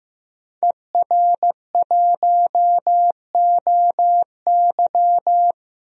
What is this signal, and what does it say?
Morse code, heard as a keyed tone, about 15 wpm, 700 Hz. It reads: ER1OY
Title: DXCC-100 Call Signs - 5 Characters - 15wpm